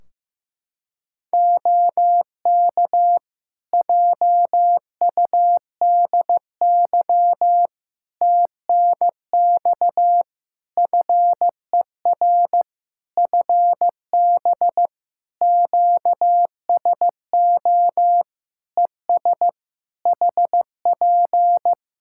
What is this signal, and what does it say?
Morse code, heard as a keyed tone, about 15 wpm, 700 Hz. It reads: OK JUDY TNX FER FB QSO ES HP
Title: Practice QSO - Exchange 4 - Ending - 15wpm